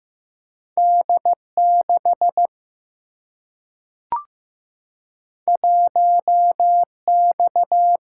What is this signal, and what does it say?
Morse code, heard as a keyed tone, about 15 wpm, 700 Hz. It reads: D 6 1 X
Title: Single-Letter Single-Number 15wpm